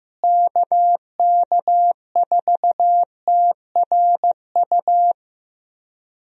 Morse code, heard as a keyed tone, about 16 wpm, 700 Hz. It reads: KK4TRU
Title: USA Call Signs - 6 Characters - Repeated 3x - 15wpm